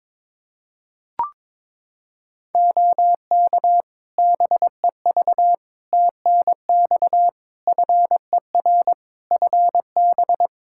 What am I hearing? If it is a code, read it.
Morse code, heard as a keyed tone, about 21 wpm, 700 Hz. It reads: OK BEV TNX FER FB
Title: Practice QSO - Exchange 4 - Ending - 22wpm